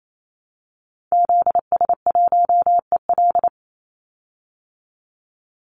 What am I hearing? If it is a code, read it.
Morse code, heard as a keyed tone, about 28 wpm, 700 Hz. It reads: ZS1EL
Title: Call Signs Repeated 2x - 28wpm